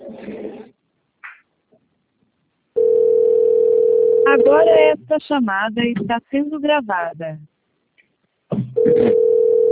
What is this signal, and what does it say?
Ringback tone (North America), 440 + 480 Hz, with speech.
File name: out- - - - - .22177